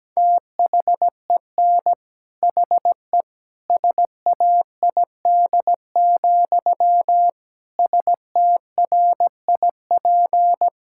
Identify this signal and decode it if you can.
Morse code, heard as a keyed tone, about 17 wpm, 700 Hz. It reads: THEN HE SAID, STRIP
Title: Sentences from Top 1600 Words 17wpm